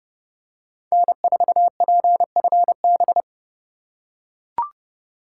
Morse code, heard as a keyed tone, about 29 wpm, 700 Hz. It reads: N4PFB